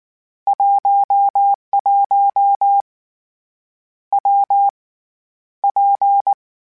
Morse code, heard as a keyed tone, about 19 wpm, 800 Hz. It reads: 11 W P
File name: Difficult Combos / Difficult Combos-mp3s / WPJ WPJ1 @18